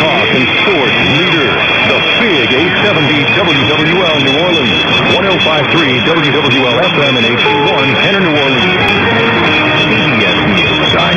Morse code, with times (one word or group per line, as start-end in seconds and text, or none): none